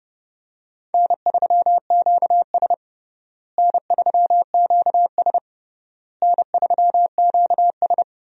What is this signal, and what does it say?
Morse code, heard as a keyed tone, about 30 wpm, 700 Hz. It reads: N3QS N3QS N3QS
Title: USA Call Signs - 4 Characters - Repeated 3x - 30wpm